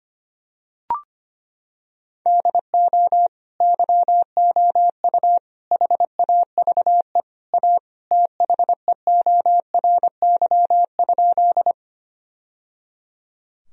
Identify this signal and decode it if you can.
Morse code, heard as a keyed tone, about 25 wpm, 700 Hz. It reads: DO YOU HAVE A THEORY?
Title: Sentences from Top 1000 Words - Review Part 3 - 25wpm